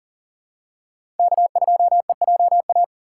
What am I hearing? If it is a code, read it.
Morse code, heard as a keyed tone, about 40 wpm, 700 Hz. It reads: K2EJA
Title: USA Call Signs - 5 Characters - 4x Character Spacing - 40wpm